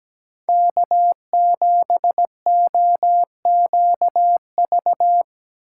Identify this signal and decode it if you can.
Morse code, heard as a keyed tone, about 20 wpm, 700 Hz. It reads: K7OQV
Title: Speed Racing - USA 1x3 Call Signs 17wpm